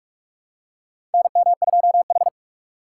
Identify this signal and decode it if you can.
Morse code, heard as a keyed tone, about 46 wpm, 700 Hz. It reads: NM2H
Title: USA Call Signs - 4 Characters - 2x Character Spacing - 45wpm